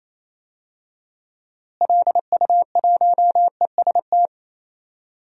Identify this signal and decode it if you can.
Morse code, heard as a keyed tone, about 28 wpm, 700 Hz. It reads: LU1EST
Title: DXCC-100 Call Signs - 6 Characters - Repeated 3x - 28wpm